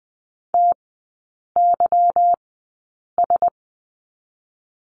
Morse code, heard as a keyed tone, about 20 wpm, 690 Hz. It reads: T Y S
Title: 145 Lesson 24e - Letter y - Three Characters 20wpm